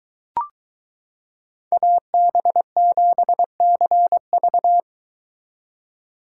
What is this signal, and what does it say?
Morse code, heard as a keyed tone, about 22 wpm, 700 Hz. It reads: AB7CV